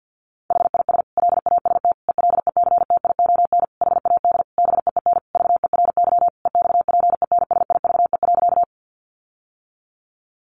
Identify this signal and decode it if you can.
Morse code, heard as a keyed tone, about 50 wpm, 700 Hz. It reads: HIS LAST ELECTION HAD BEEN VERY EXPENSIVE.
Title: Sentences from Top 1500 Words 50wpm